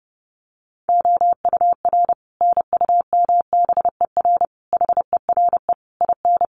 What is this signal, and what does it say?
Morse code, heard as a keyed tone, about 30 wpm, 700 Hz. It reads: OUR NUMBER HERE IN
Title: Sets of 5 Words Encore2 30wpm